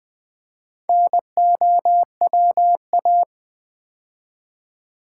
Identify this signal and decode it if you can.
Morse code, heard as a keyed tone, about 20 wpm, 700 Hz. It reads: NOWA
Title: Build A Word - 2 Letters At A Time - 8-Letter Words - 20wpm